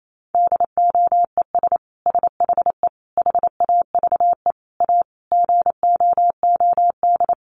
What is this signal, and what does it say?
Morse code, heard as a keyed tone, about 28 wpm, 700 Hz. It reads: DOES SHE HAVE A GOOD
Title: Sentences from Top 1400 Words 28wpm